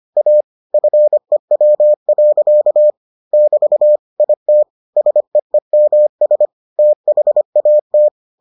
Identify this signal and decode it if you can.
Morse code, heard as a keyed tone, about 25 wpm, 600 Hz. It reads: A FEW. <BT> IT SEEMS THAT